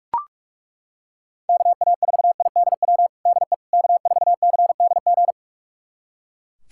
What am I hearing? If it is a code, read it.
Morse code, heard as a keyed tone, about 42 wpm, 700 Hz. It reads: KA4IDW DE K4CDG
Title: USA Callsign DE Callsign 45wpm